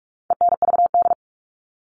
Morse code, heard as a keyed tone, about 44 wpm, 700 Hz. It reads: ENVD
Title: All Cut Numbers - 4-Digit 45wpm